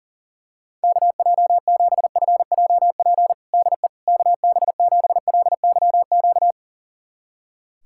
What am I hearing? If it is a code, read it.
Morse code, heard as a keyed tone, about 40 wpm, 700 Hz. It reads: KJ7FJP DE KB7LYQ
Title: USA Callsign DE Callsign - 6 Characters - 40wpm